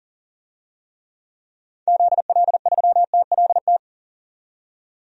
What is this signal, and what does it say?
Morse code, heard as a keyed tone, about 40 wpm, 700 Hz. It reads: ZL3TLT
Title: DXCC-100 Call Signs - 6 Characters - 40wpm